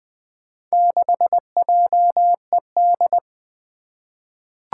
Morse code, heard as a keyed tone, about 20 wpm, 700 Hz. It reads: 6JED
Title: Four Challenging Letter-Number Combinations 20wpm